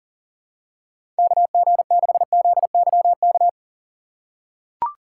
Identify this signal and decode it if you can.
Morse code, heard as a keyed tone, about 40 wpm, 700 Hz. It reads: KG6ZYK
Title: USA Call Signs - 6 Characters - 4x Character Spacing - 40wpm